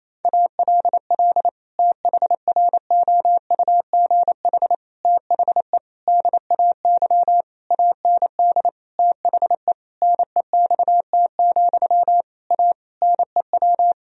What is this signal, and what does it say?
Morse code, heard as a keyed tone, about 28 wpm, 700 Hz. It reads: ALL THROUGH THE DAY AND THE NEXT, A NEW
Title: Sentences from Top 1000 Words - Review Part 1 - 28wpm